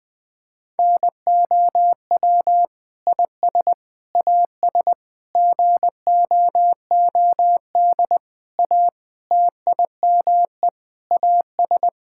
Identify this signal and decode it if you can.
Morse code, heard as a keyed tone, about 20 wpm, 700 Hz. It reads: NOW IS AS GOOD A TIME AS
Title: Sentences with 8-Word Phrases - Spoken Beginning - 20wpm